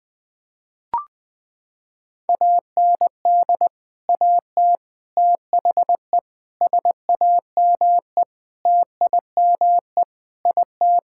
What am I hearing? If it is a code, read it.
Morse code, heard as a keyed tone, about 21 wpm, 700 Hz. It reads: AND AT THE SAME TIME IT
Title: Sentences with 5-Word Phrases - 20wpm